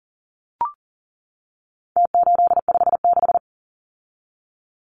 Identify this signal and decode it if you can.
Morse code, heard as a keyed tone, about 40 wpm, 700 Hz. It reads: T856
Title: Numbers - Contest Style - 4-Digit 40wpm